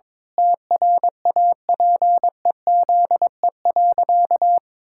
Morse code, heard as a keyed tone, about 22 wpm, 700 Hz. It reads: TRAPEZE.
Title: Pangram Challenge v2 - 22wpm